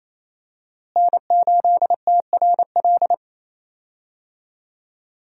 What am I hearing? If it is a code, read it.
Morse code, heard as a keyed tone, about 28 wpm, 700 Hz. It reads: N8TRL
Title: USA Call Signs - 5 Characters - Repeated 2x - 28wpm